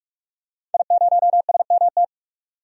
Morse code, heard as a keyed tone, about 46 wpm, 700 Hz. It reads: I0SMT
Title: DXCC-100 Call Signs - 5 Characters - 45wpm